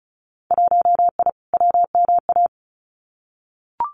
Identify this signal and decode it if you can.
Morse code, heard as a keyed tone, about 36 wpm, 700 Hz. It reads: 1I WMA
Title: Winter Field Day Exchanges Encore 35wpm